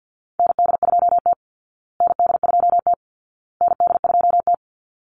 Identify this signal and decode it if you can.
Morse code, heard as a keyed tone, about 50 wpm, 700 Hz. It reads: ND2T ND2T ND2T
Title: Call Signs Repeated 2x - 50wpm